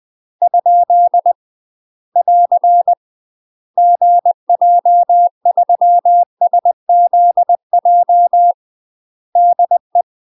Morse code, heard as a keyed tone, about 20 wpm, 700 Hz. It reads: ? <AR> GJ3SZJ DE
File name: UK MASTER QSOs TRAINING / 20WPM / UK MASTER QSOs 0027